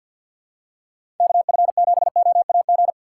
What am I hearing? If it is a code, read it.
Morse code, heard as a keyed tone, about 53 wpm, 700 Hz. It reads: KF7OAG
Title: USA Call Signs - 6 Characters - 3x Character Spacing - 50wpm